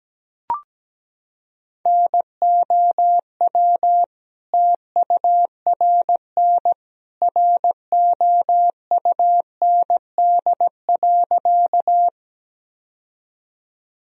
Morse code, heard as a keyed tone, about 17 wpm, 700 Hz. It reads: NOW TURN ROUND.
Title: Sentences from Top 1000 Words - Review Part 1 - 17wpm